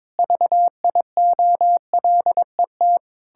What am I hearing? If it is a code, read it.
Morse code, heard as a keyed tone, about 22 wpm, 700 Hz. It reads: VIOLET
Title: Pangram Challenge v2 - 22wpm